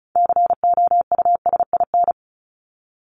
Morse code, heard as a keyed tone, about 34 wpm, 700 Hz. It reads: COUSIN
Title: Top 1501-1750 Words 35wpm